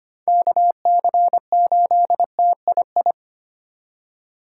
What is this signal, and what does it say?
Morse code, heard as a keyed tone, about 34 wpm, spaced as for 30 wpm, 700 Hz. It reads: KC8TII
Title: Speed Racing - USA 2x3 Call Signs 25wpm